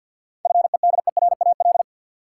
Morse code, heard as a keyed tone, about 50 wpm, 700 Hz. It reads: FEDERAL
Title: Sentences from Top 1100 Words - New Word Drill Alt - 50wpm